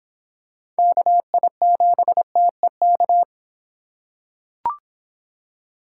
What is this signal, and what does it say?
Morse code, heard as a keyed tone, about 24 wpm, 700 Hz. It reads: KI7TEK